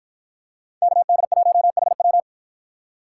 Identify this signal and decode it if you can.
Morse code, heard as a keyed tone, about 52 wpm, 700 Hz. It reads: KD1HW